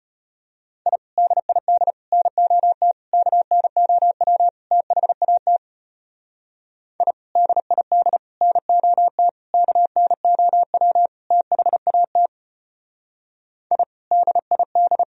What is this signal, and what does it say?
Morse code, heard as a keyed tone, about 30 wpm, 700 Hz. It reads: I DID NOT KNOW THAT I DID NOT KNOW THAT I DID
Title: Speed-Racing - Sets of 5 Words 25wpm